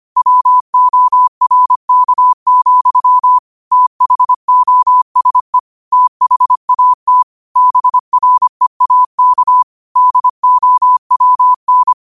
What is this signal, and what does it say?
Morse code, heard as a keyed tone, about 25 wpm, 1000 Hz. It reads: WORK, THOSE THAT BREAK DOWN
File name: qotdfiles / 25wpm